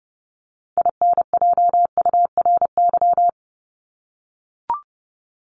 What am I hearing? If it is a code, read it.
Morse code, heard as a keyed tone, about 30 wpm, 700 Hz. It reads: INJURY
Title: Top 1751-2000 Words 30wpm